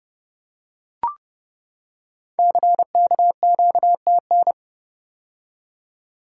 Morse code, heard as a keyed tone, about 30 wpm, 700 Hz. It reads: CKQTN
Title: Code Groups - 5 Letters - 30wpm